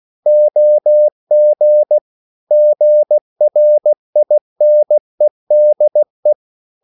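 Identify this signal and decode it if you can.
Morse code, heard as a keyed tone, about 16 wpm, 600 Hz. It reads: OG GRINEDE
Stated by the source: Slow (QRS) Morse code at 16 wpm, from CW med Gnister.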